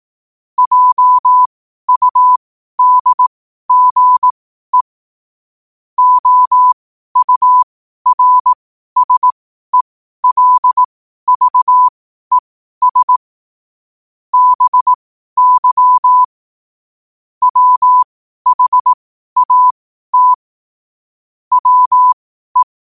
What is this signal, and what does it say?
Morse code, heard as a keyed tone, about 18 wpm, spaced as for 12 wpm, 1000 Hz. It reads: JUDGE OURSELVES BY WHAT WE